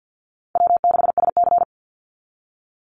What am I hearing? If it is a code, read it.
Morse code, heard as a keyed tone, about 50 wpm, 700 Hz. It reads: W6SC
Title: Call Signs Repeated 3x - 50wpm